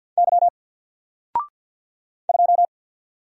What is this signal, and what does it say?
Morse code, heard as a keyed tone, about 48 wpm, 700 Hz. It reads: Y 2
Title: Single Character (Letter:Number:Punctuation:Prosign) Alt 50wpm